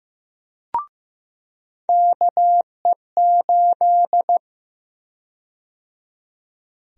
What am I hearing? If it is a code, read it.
Morse code, heard as a keyed tone, about 15 wpm, 700 Hz. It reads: KE8
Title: USA Call Sign Prefixes - 15wpm